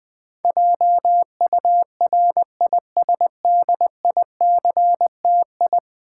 Morse code, heard as a keyed tone, about 20 wpm, 700 Hz. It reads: JURISDICTI
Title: Build A Word - 5 Letters At A Time - 12-Letter Words - 20wpm